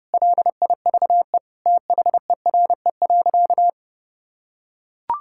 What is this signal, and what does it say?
Morse code, heard as a keyed tone, about 30 wpm, 700 Hz. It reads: LIVE THERE.
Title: Sentences from Top 1000 Words - Review Part 6 - 30wpm